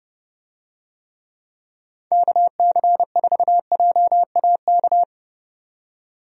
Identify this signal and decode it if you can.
Morse code, heard as a keyed tone, about 30 wpm, 700 Hz. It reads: KC4JAK
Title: USA Call Signs - 6 Characters - 30wpm